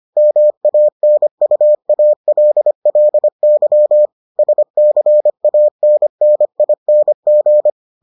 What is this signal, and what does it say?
Morse code, heard as a keyed tone, about 25 wpm, 600 Hz. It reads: MANUALLY SCANNING